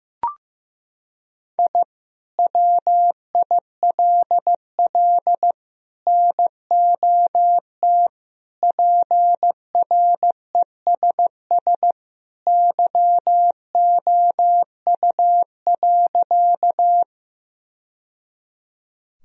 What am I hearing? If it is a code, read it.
Morse code, heard as a keyed tone, about 15 wpm, 700 Hz. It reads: I WILL NOT PRESS YOU.
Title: Sentences from Top 1000 Words - Review Part 1 - 15wpm